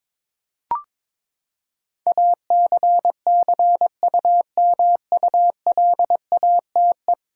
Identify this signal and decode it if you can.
Morse code, heard as a keyed tone, about 22 wpm, 700 Hz. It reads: ACCUMULATE
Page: morse-code-ninja-practice-22wpm